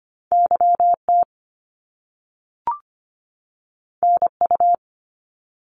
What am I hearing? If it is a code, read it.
Morse code, heard as a keyed tone, about 25 wpm, 700 Hz. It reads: YT NU
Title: Canadian Provinces and Territories 25wpm